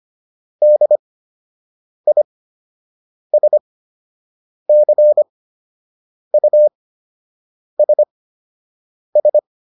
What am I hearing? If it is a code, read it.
Morse code, heard as a keyed tone, about 25 wpm, spaced as for 7 wpm, 600 Hz. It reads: DISCUSS